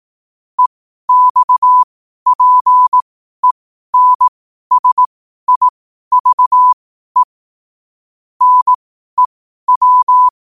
Morse code, heard as a keyed tone, about 18 wpm, spaced as for 12 wpm, 1000 Hz. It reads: EXPENSIVE NEW
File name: qotdfiles / 12wpm